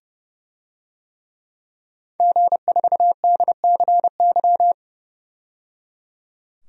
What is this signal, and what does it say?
Morse code, heard as a keyed tone, about 30 wpm, 690 Hz. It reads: G4DCY
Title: DXCC-100 Call Signs - 5 Characters - 30wpm